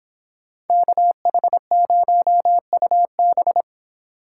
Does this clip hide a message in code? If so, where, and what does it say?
Yes: Morse code from 0.7 to 3.6 s, heard as a keyed tone, about 24 wpm, 700 Hz. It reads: KH0UB